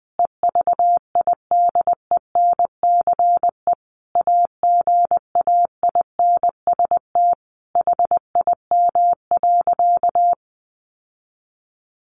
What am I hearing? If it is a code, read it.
Morse code, heard as a keyed tone, about 20 wpm, 700 Hz. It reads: EVIDENCE AGAINST HIM.
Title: Sentences from Top 1300 Words 20wpm